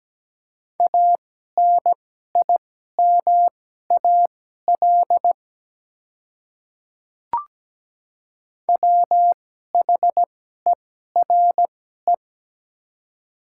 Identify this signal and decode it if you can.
Morse code, heard as a keyed tone, about 17 wpm, spaced as for 12 wpm, 700 Hz. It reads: ANIMAL WHERE
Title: 2x Character Spacing - Top 200 Words - 17wpm